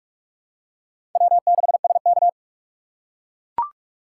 Morse code, heard as a keyed tone, about 44 wpm, 700 Hz. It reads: W6SK